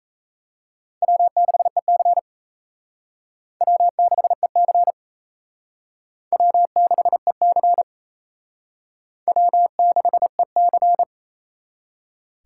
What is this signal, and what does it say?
Morse code, heard as a keyed tone, about 36 wpm, 700 Hz. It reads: W6EC W6EC W6EC W6EC